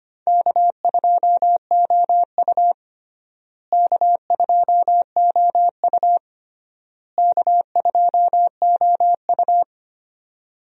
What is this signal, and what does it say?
Morse code, heard as a keyed tone, about 25 wpm, 700 Hz. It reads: K2OU K2OU K2OU
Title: USA Call Signs - 4 Characters - Repeated 3x - 25wpm